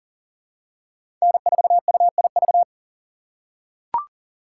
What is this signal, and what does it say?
Morse code, heard as a keyed tone, about 40 wpm, 700 Hz. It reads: N4UIV
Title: USA Call Signs - 3x Character Spacing - 40wpm